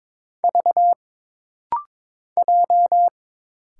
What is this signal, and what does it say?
Morse code, heard as a keyed tone, about 22 wpm, 700 Hz. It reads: V J
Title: Single Letter-Number Alt - Rapid-Fire 22wpm